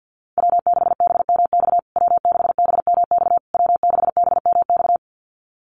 Morse code, heard as a keyed tone, about 49 wpm, 700 Hz. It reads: W6BMX W6BMX W6BMX
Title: USA Call Signs Repeated 3x 50wpm